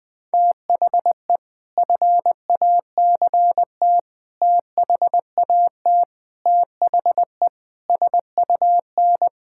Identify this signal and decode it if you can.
Morse code, heard as a keyed tone, about 20 wpm, 700 Hz. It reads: THE FACT THAT THE SUN
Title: Sentences with 4-Word Phrases - 20wpm